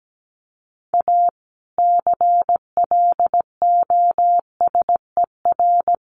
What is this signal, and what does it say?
Morse code, heard as a keyed tone, about 17 wpm, 700 Hz. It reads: A CLOSER
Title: Sets of 5 Words Encore 17wpm